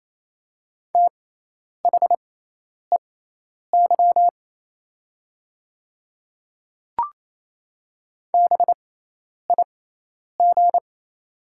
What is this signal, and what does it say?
Morse code, heard as a keyed tone, about 28 wpm, spaced as for 10 wpm, 700 Hz. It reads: THEY BIG
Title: 6x Character Spacing - Top 200 Words - 28wpm